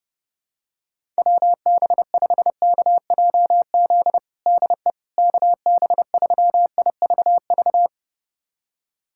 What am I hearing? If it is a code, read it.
Morse code, heard as a keyed tone, about 30 wpm, 700 Hz. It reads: WB5KJZ DE KB3IVV
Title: USA Callsign DE Callsign - 6 Characters - 30wpm